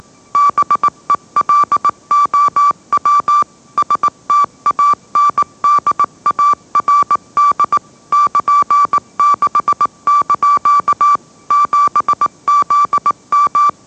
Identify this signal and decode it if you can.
Morse code, heard as a keyed tone, about 18 wpm, 1160 Hz. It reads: BELOW STANDARD <KN>6) 7ZM
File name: mrx / mrx081 20wpm